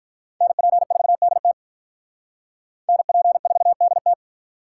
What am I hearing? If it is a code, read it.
Morse code, heard as a keyed tone, about 44 wpm, 700 Hz. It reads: NP4DT NP4DT